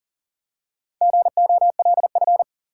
Morse code, heard as a keyed tone, about 40 wpm, 700 Hz. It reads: GOLF
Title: Four Letter Words - 40wpm